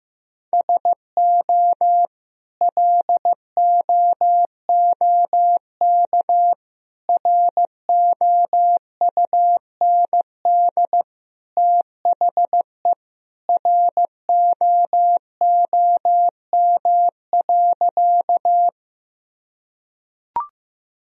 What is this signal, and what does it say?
Morse code, heard as a keyed tone, about 15 wpm, 700 Hz. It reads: SO LOOK ROUND THE ROOM.
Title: Sentences from Top 1000 Words - Review Part 1 - 15wpm